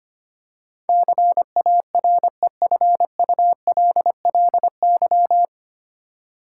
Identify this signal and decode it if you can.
Morse code, heard as a keyed tone, about 25 wpm, 700 Hz. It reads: CAREFULLY
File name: Nine Letter Words - 25wpm